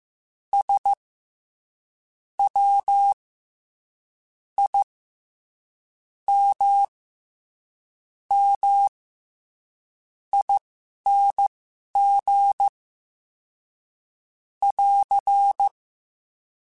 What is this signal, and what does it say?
Morse code, heard as a keyed tone, about 15 wpm, spaced as for 11 wpm, 800 Hz. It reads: S W I M M ING <AR>
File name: Special INGs-mp3s / Special INGs-mp3s / SlowToFaster ing @10